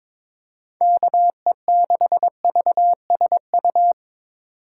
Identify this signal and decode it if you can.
Morse code, heard as a keyed tone, about 22 wpm, 700 Hz. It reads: KE6VSU
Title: USA Call Signs - 6 Characters - 3x Character Spacing - 22wpm